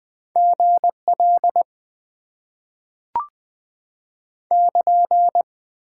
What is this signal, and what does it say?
Morse code, heard as a keyed tone, about 20 wpm, 700 Hz. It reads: GL <KN>
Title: Top 90 QSO Elements - 8x Character Spacing - 20wpm